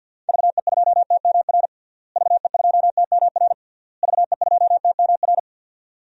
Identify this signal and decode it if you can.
Morse code, heard as a keyed tone, about 52 wpm, 700 Hz. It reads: VE2TMR VE2TMR VE2TMR
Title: DXCC-100 Call Signs - 6 Characters - Repeated 3x - 50wpm